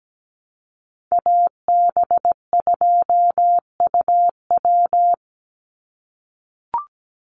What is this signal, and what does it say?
Morse code, heard as a keyed tone, about 17 wpm, 700 Hz. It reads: AB2UW
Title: USA Call Signs 17wpm